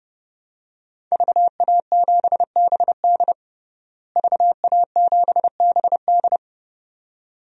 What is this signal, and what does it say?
Morse code, heard as a keyed tone, about 32 wpm, 700 Hz. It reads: VA7BD VA7BD
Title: DXCC-100 Call Signs - 5 Characters - Repeated 2x - 30wpm